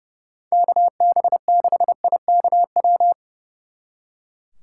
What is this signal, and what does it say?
Morse code, heard as a keyed tone, about 37 wpm, 700 Hz. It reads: KB6IKW
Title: Speed Racing - USA 2x3 Call Signs 30wpm